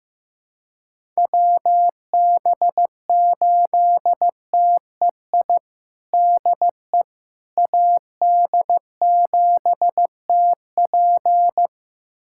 Morse code, heard as a keyed tone, about 16 wpm, 700 Hz. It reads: WB8TEI DE AD7TP
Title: USA Callsign DE Callsign 15wpm